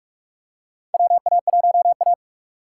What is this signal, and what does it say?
Morse code, heard as a keyed tone, about 44 wpm, 700 Hz. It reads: WA1A
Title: USA Call Signs - 4 Characters - 6x Character Spacing - 45wpm